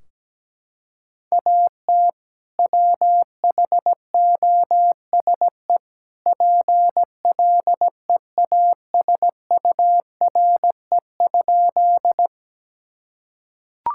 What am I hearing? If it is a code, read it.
Morse code, heard as a keyed tone, about 17 wpm, 700 Hz. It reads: AT WHOSE PLEASURE?
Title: Sentences from Top 1600 Words 17wpm